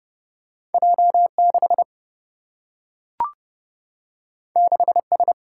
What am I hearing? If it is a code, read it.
Morse code, heard as a keyed tone, about 30 wpm, 700 Hz. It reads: J 6 6 S
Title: Single-Letter Single-Number 30wpm